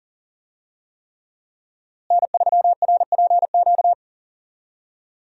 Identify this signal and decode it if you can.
Morse code, heard as a keyed tone, about 40 wpm, 700 Hz. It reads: N3RPQ
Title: USA Call Signs - 5 Characters - Repeated 3x - 40wpm